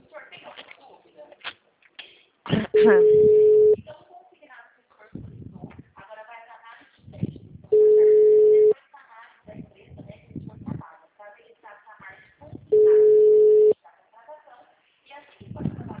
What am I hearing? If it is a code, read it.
Ringback tone (Europe, 425 Hz), with speech.
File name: out- -2023- - - .34